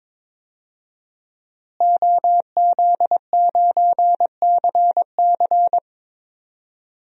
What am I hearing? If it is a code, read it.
Morse code, heard as a keyed tone, about 22 wpm, 700 Hz. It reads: OZ9CC
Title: DXCC-100 Call Signs - 5 Characters - Repeated 2x - 22wpm